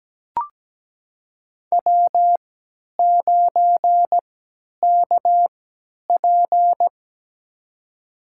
Morse code, heard as a keyed tone, about 17 wpm, spaced as for 10 wpm, 700 Hz. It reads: W9KP
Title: USA Call Signs - 4 Characters - 3x Character Spacing - 17wpm